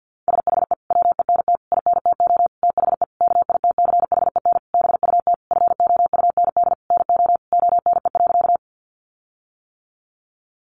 Morse code, heard as a keyed tone, about 48 wpm, 700 Hz. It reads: SHE WENT INTO THE KITCHEN BUT FOUND NO ONE.
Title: Sentences from Top 1400 Words 50wpm